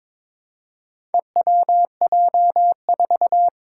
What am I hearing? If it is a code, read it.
Morse code, heard as a keyed tone, about 22 wpm, 700 Hz. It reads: EWJ4
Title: Four Challenging Letter-Number Combinations 22wpm